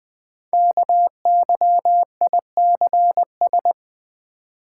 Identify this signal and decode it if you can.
Morse code, heard as a keyed tone, about 20 wpm, 700 Hz. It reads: KYICS
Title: Code Groups - 5 Letters - 20wpm